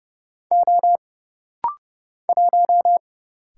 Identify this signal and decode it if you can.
Morse code, heard as a keyed tone, about 30 wpm, 700 Hz. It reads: O 1
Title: Single Letter-Number Alt - Rapid-Fire 30wpm